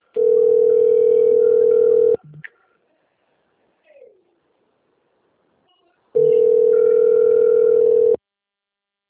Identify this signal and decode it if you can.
Ringback tone (North America), 440 + 480 Hz.